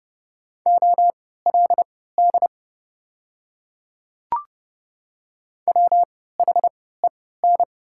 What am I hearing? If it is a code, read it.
Morse code, heard as a keyed tone, about 30 wpm, spaced as for 17 wpm, 700 Hz. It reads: OLD WHEN
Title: 3x Character Spacing - Top 200 Words - 30wpm